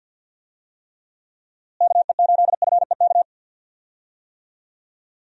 Morse code, heard as a keyed tone, about 50 wpm, 700 Hz. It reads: KE8FEK
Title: USA Call Signs - 6 Characters - Repeated 3x - 50wpm